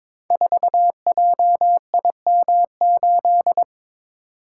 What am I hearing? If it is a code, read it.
Morse code, heard as a keyed tone, about 22 wpm, 700 Hz. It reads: 4JIM8
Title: Code Groups - 5 Letter:Numbers - 22wpm